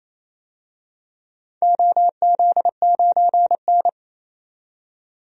Morse code, heard as a keyed tone, about 28 wpm, 700 Hz. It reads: OZ9N
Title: DXCC-100 Call Signs - 4 Characters - 28wpm